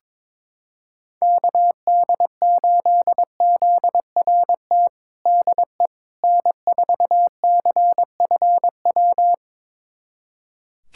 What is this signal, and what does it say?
Morse code, heard as a keyed tone, about 22 wpm, 700 Hz. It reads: KD8ZRT DE N4CFW
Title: USA Callsign DE Callsign 22wpm